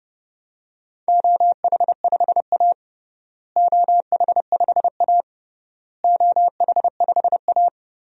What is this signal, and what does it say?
Morse code, heard as a keyed tone, about 30 wpm, 700 Hz. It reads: OH5A OH5A OH5A
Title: DXCC-100 Call Signs - 4 Characters - Repeated 3x - 30wpm